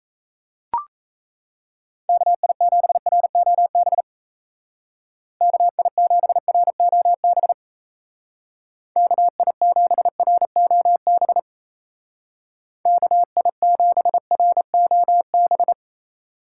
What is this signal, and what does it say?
Morse code, heard as a keyed tone, about 34 wpm, 700 Hz. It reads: KI7ROB KI7ROB KI7ROB KI7ROB
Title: Speed Racing - USA 2x3 Call Signs 28wpm